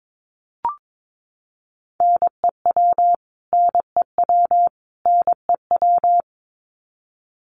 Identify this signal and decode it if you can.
Morse code, heard as a keyed tone, about 22 wpm, 700 Hz. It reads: NEW NEW NEW
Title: Top 500 Words Repeated 22wpm